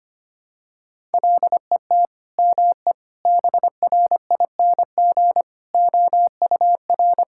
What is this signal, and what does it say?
Morse code, heard as a keyed tone, about 25 wpm, 700 Hz. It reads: LET ME BRING OUR
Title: Sentences from Top 1000 Words - Review Part 6 - 25wpm